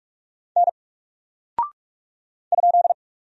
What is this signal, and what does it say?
Morse code, heard as a keyed tone, about 44 wpm, 700 Hz. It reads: N ?
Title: Single Character (Letter:Number:Punctuation:Prosign) 45wpm